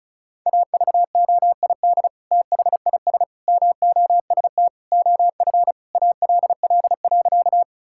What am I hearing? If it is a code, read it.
Morse code, heard as a keyed tone, about 36 wpm, 700 Hz. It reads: AVOID THIS MOST OF ALL.
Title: Sentences from Top 1000 Words - Review Part 6 - 35wpm